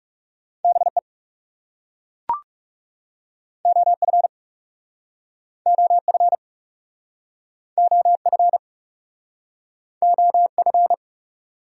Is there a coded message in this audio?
Yes: Morse code from 0.6 to 10.9 s, heard as a keyed tone, about 39 wpm, 700 Hz. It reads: BE OF OF OF OF